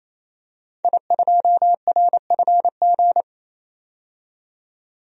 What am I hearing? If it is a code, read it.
Morse code, heard as a keyed tone, about 28 wpm, 700 Hz. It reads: I2RFG
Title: DXCC-100 Call Signs - 5 Characters - 28wpm